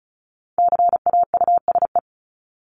Morse code, heard as a keyed tone, about 34 wpm, 700 Hz. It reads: CAUSE
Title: ED Suffix - Top 100 Words with Base Words 35wpm